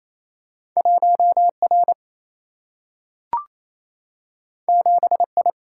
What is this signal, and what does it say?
Morse code, heard as a keyed tone, about 28 wpm, 700 Hz. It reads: 1 R 7 I
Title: Single-Letter Single-Number 28wpm